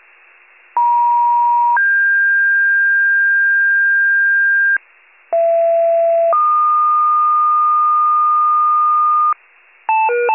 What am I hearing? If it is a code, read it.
Two-tone page, heard as a one-second tone, then a three-second tone, about 950 then 1600 Hz, about 670 then 1220 Hz, with noise. It is heard from 0.8 to 9.3 s.
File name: pager / 39 RESCUE / wcalert